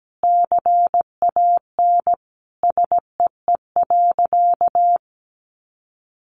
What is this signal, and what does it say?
Morse code, heard as a keyed tone, about 17 wpm, 700 Hz. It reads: CAN SEE.
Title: Sentences from Top 1000 Words 17wpm